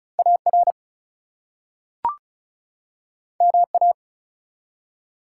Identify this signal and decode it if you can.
Morse code, heard as a keyed tone, about 36 wpm, 700 Hz. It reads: AR MA
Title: US States and Canadian Provinces 35wpm